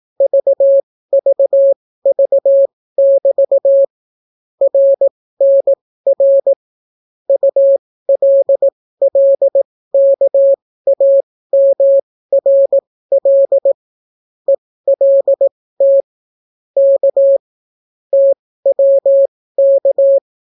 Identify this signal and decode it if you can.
Morse code, heard as a keyed tone, about 18 wpm, spaced as for 14 wpm, 550 Hz. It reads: VVV<BT> RNR ULLKAMRL ELT K TWK